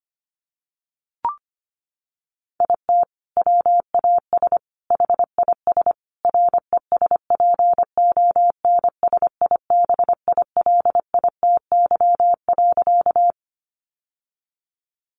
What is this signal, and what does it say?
Morse code, heard as a keyed tone, about 25 wpm, 700 Hz. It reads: IT WAS HIS RESPONSIBILITY.
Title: Sentences from Top 1200 Words 25wpm